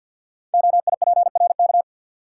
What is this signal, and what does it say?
Morse code, heard as a keyed tone, about 53 wpm, 700 Hz. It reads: OIPRK
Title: Code Groups - 5 Letters - 50wpm